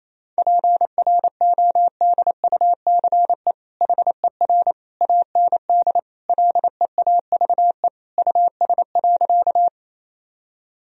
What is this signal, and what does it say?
Morse code, heard as a keyed tone, about 28 wpm, 700 Hz. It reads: PRODUCE HER AND LEAVE US.
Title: Sentences from Top 1000 Words - Review Part 3 - 28wpm